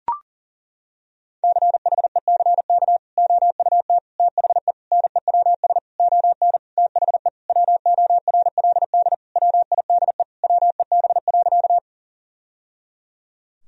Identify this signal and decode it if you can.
Morse code, heard as a keyed tone, about 37 wpm, 700 Hz. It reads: CHECK OUT THE NEWS ON THE WORLD WIDE WEB.
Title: Sentences from Top 1000 Words - Review Part 4 - 40wpm